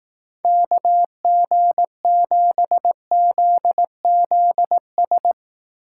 Morse code, heard as a keyed tone, about 18 wpm, 700 Hz. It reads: KG7ZZS
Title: Speed Racing - USA 2x3 Call Signs 15wpm